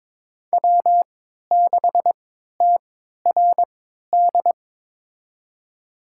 Morse code, heard as a keyed tone, about 22 wpm, spaced as for 13 wpm, 700 Hz. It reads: W6TRD
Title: USA Call Signs - 3x Character Spacing - 22wpm